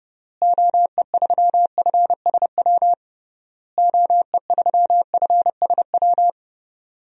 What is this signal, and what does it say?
Morse code, heard as a keyed tone, about 30 wpm, 700 Hz. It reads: OE3FSW OE3FSW
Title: DXCC-100 Call Signs - 6 Characters - Repeated 3x - 30wpm